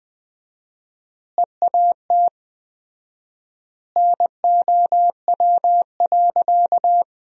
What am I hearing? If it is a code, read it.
Morse code, heard as a keyed tone, about 20 wpm, 700 Hz. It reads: EAT NOW.
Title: 4x Word-Spacing - Sentences from Top 300 Words - 20wpm